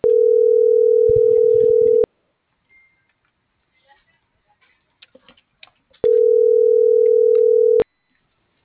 Ringback tone (North America), 440 + 480 Hz.